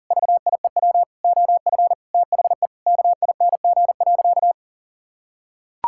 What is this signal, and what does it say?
Morse code, heard as a keyed tone, about 37 wpm, 700 Hz. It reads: VIEW OF THE KING.
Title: Sentences from Top 1000 Words - Review Part 6 - 40wpm